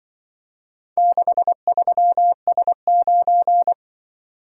Morse code, heard as a keyed tone, about 24 wpm, 700 Hz. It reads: 63S9
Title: Speed-Racing - Four Letter-Number Combinations - 20wpm